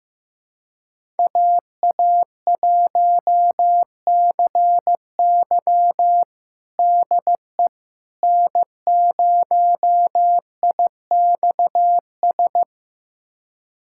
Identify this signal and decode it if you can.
Morse code, heard as a keyed tone, about 15 wpm, 700 Hz. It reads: AA1CY DE N0IXS
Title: USA Callsign DE Callsign - 5 Characters - 15wpm